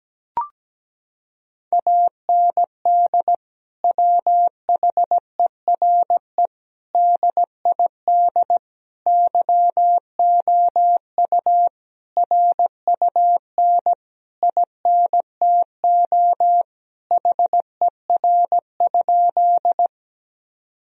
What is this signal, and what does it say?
Morse code, heard as a keyed tone, about 17 wpm, 700 Hz. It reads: AND WHERE DID YOU RUN INTO HER?
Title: Sentences from Top 1000 Words - Review Part 1 - 17wpm